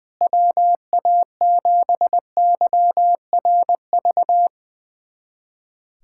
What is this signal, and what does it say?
Morse code, heard as a keyed tone, about 20 wpm, 700 Hz. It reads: WA7YRV
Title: USA Callsign DE Callsign 20wpm